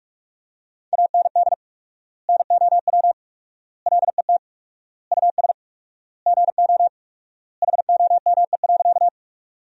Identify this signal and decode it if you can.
Morse code, heard as a keyed tone, about 45 wpm, 700 Hz. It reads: AND NOW LET US GO HOME.
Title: 4x Word-Spacing - Sentences from Top 300 Words - 45wpm